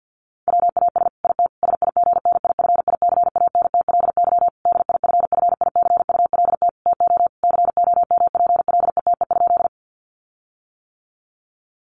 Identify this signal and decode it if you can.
Morse code, heard as a keyed tone, about 50 wpm, 700 Hz. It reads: WAS IT SIGNIFICANTLY DIFFICULT TO COMPLETE?
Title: Sentences from Top 1500 Words 50wpm